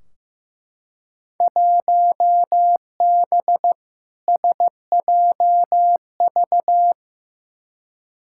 Morse code, heard as a keyed tone, about 15 wpm, 700 Hz. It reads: 1B SJV
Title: Field Day Exchanges Encore - 15wpm